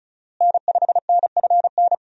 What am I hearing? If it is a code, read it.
Morse code, heard as a keyed tone, about 36 wpm, 700 Hz. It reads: N5NFN
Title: USA Call Signs - 5 Characters - Repeated 3x - 35wpm